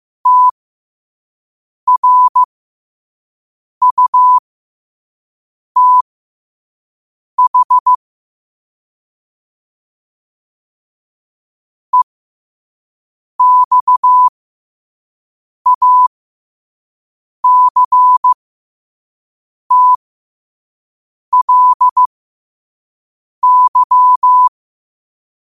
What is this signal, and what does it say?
Morse code, heard as a keyed tone, about 15 wpm, spaced as for 5 wpm, 1000 Hz. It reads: TRUTH EXACTLY
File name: qotdfiles / 5wpm